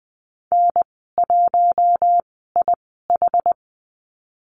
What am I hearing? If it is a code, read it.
Morse code, heard as a keyed tone, about 20 wpm, spaced as for 15 wpm, 700 Hz. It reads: N1IH
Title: 140 Lesson 23g - Letter f - Call Signs 20wpm